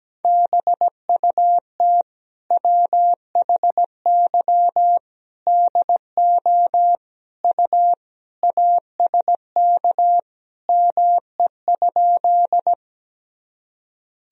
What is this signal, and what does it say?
Morse code, heard as a keyed tone, about 17 wpm, 700 Hz. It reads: BUT WHY DO U ASK ME?
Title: Sentences from Top 200 Words and Abbreviations 17wpm